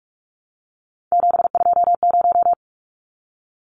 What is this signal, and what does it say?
Morse code, heard as a keyed tone, about 44 wpm, 700 Hz. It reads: 720
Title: Numbers - 3-Digit 45wpm